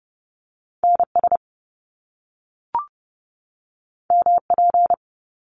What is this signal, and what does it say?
Morse code, heard as a keyed tone, about 30 wpm, 700 Hz. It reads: NS MP
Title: Top Two Letter Combinations in Words 30wpm